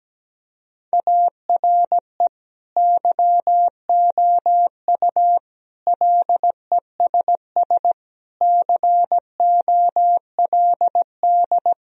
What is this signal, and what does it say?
Morse code, heard as a keyed tone, about 17 wpm, 700 Hz. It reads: ARE YOU LESS COLD
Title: Sentences from Top 1000 Words - Review Part 2 - 17wpm